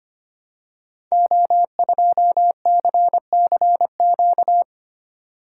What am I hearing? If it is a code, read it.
Morse code, heard as a keyed tone, about 25 wpm, 700 Hz. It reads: O2CCQ
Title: Code Groups - 5 Letter:Numbers - 25wpm